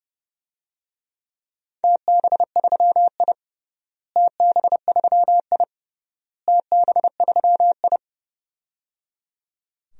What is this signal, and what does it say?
Morse code, heard as a keyed tone, about 29 wpm, 700 Hz. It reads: TB3I TB3I TB3I
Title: DXCC-100 Call Signs - 4 Characters - Repeated 3x - 30wpm